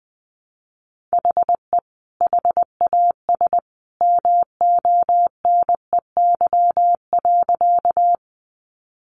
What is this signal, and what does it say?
Morse code, heard as a keyed tone, about 20 wpm, 700 Hz. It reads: HE HAS MONEY.
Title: Sentences from Top 400 Words 20wpm